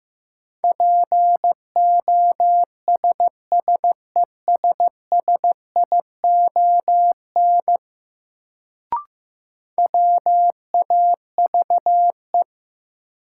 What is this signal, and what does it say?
Morse code, heard as a keyed tone, about 15 wpm, 700 Hz. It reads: POSSESSION WAVE
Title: Sentences from Top 1400 Words - New Word Drill Alt - 15wpm